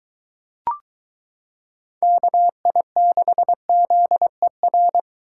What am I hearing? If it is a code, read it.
Morse code, heard as a keyed tone, about 23 wpm, 700 Hz. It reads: KI6ZER